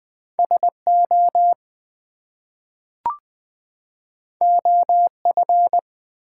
Morse code, heard as a keyed tone, about 20 wpm, 700 Hz. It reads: SO OF
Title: Two Letter Words - 20wpm